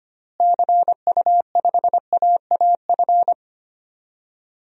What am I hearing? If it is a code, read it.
Morse code, heard as a keyed tone, about 25 wpm, 700 Hz. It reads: CU5AAF
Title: DXCC-100 Call Signs - 6 Characters - Repeated 3x - 25wpm